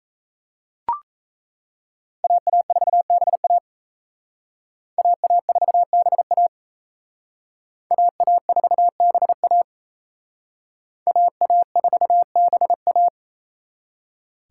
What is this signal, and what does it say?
Morse code, heard as a keyed tone, about 35 wpm, 700 Hz. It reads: AA4BA AA4BA AA4BA AA4BA